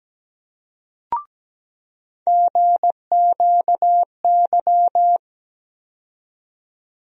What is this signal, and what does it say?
Morse code, heard as a keyed tone, about 17 wpm, 700 Hz. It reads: GQY
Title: USA Call Sign Postfixes - 3 Letters 17wpm